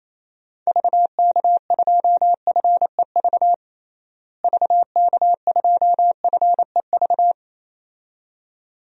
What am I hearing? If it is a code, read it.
Morse code, heard as a keyed tone, about 27 wpm, 700 Hz. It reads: VK2FEV VK2FEV
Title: DXCC-100 Call Signs - 6 Characters - Repeated 2x - 28wpm